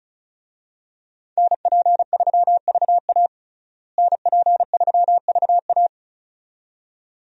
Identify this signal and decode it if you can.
Morse code, heard as a keyed tone, about 36 wpm, 700 Hz. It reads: NP3VA NP3VA
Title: USA Call Signs - 5 Characters - Repeated 2x - 35wpm